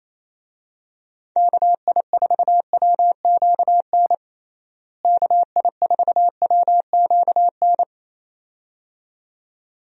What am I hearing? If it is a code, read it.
Morse code, heard as a keyed tone, about 28 wpm, 700 Hz. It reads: KI4WQN KI4WQN
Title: USA Call Signs - 6 Characters - Repeated 2x - 28wpm